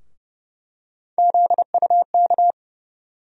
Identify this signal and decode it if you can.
Morse code, heard as a keyed tone, about 30 wpm, 700 Hz. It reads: ZUK
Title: USA Call Sign Postfixes - 3 Letters 30wpm